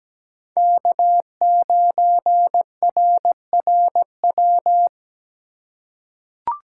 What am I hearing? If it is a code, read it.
Morse code, heard as a keyed tone, about 17 wpm, 700 Hz. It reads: K9RRW
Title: USA Call Signs - 5 Characters - 4x Character Spacing - 17wpm